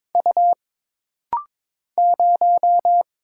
Morse code, heard as a keyed tone, about 22 wpm, 700 Hz. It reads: U 0
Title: Single Letter-Number Alt - Rapid-Fire 22wpm